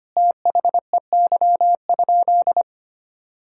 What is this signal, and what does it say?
Morse code, heard as a keyed tone, about 25 wpm, 700 Hz. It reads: THEY?
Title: Sentences from Top 1000 Words - Review Part 5 - 25wpm